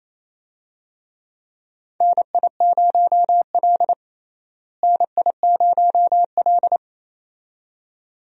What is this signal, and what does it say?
Morse code, heard as a keyed tone, about 28 wpm, 700 Hz. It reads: NI0L NI0L
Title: USA Call Signs - 4 Characters - Repeated 2x - 28wpm